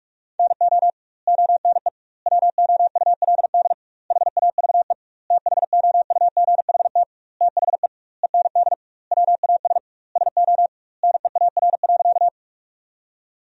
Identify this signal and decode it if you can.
Morse code, heard as a keyed tone, about 41 wpm, 700 Hz. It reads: NO ONE WOULD HAVE THOUGHT THE END WAS SO NEAR.
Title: 2x Word-Spacing - Sentences from Top 300 Words - 45wpm